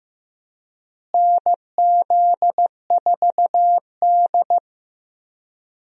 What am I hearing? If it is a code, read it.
Morse code, heard as a keyed tone, about 15 wpm, 700 Hz. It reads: NZ4D
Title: USA Call Signs - 4 Characters - Repeated 2x - 15wpm